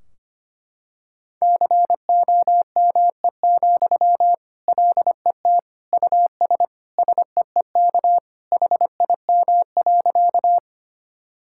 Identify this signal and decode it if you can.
Morse code, heard as a keyed tone, about 25 wpm, 700 Hz. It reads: COME, LET US SEEK HIM.
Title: Sentences from Top 1600 Words 25wpm